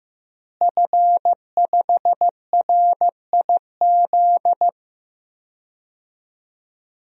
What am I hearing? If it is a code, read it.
Morse code, heard as a keyed tone, about 15 wpm, 700 Hz. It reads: F5RIZ
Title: DXCC-100 Call Signs - 5 Characters - Repeated 3x - 15wpm